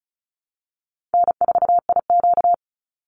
Morse code, heard as a keyed tone, about 34 wpm, 700 Hz. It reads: N4IQ
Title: Call Signs Repeated 3x - 35wpm